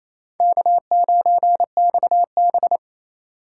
Morse code, heard as a keyed tone, about 27 wpm, 700 Hz. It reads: K9XB
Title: USA Call Signs - 2x Character Spacing - 28wpm